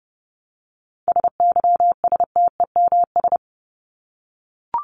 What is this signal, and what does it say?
Morse code, heard as a keyed tone, about 30 wpm, 700 Hz. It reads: SYSTEMS
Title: ES:S Suffix - Top 100 Words with Base Words 30wpm